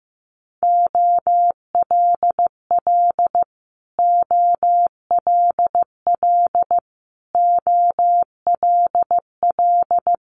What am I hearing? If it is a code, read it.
Morse code, heard as a keyed tone, about 15 wpm, 700 Hz. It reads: OLL OLL OLL
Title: Multi-letter Phonemes Repeated 3x 15wpm